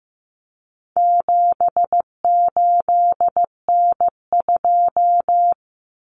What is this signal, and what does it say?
Morse code, heard as a keyed tone, about 15 wpm, 700 Hz. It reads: 78N2
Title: Common Cut Numbers - 4-Digit 15wpm